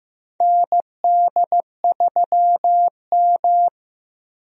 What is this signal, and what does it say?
Morse code, heard as a keyed tone, about 15 wpm, 700 Hz. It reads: ND3M
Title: USA Call Signs - 4 Characters - Repeated 3x - 15wpm